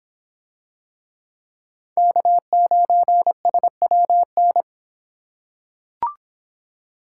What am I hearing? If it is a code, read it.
Morse code, heard as a keyed tone, about 26 wpm, 700 Hz. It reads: K9SWN